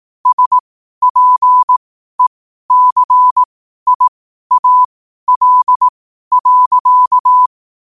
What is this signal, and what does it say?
Morse code, heard as a keyed tone, about 18 wpm, spaced as for 12 wpm, 1000 Hz. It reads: SPECIAL.
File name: qotdfiles / 12wpm